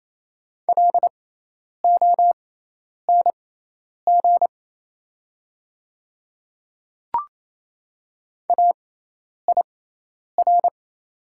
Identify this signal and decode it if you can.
Morse code, heard as a keyed tone, about 28 wpm, spaced as for 10 wpm, 690 Hz. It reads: LONG AIR
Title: 6x Character Spacing - Top 200 Words - 28wpm